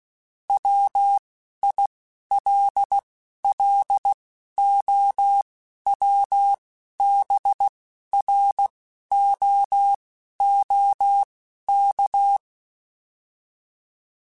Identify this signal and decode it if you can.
Morse code, heard as a keyed tone, about 16 wpm, spaced as for 11 wpm, 800 Hz. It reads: WILLOWBROOK